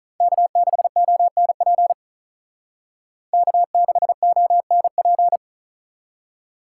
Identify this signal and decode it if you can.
Morse code, heard as a keyed tone, about 42 wpm, 700 Hz. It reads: K6ONP K6ONP
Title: Speed Racing - USA 1x3 Call Signs 35wpm